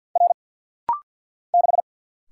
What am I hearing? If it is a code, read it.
Morse code, heard as a keyed tone, about 50 wpm, 700 Hz. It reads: R 6
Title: Single Letter-Number Alt - Rapid-Fire 50wpm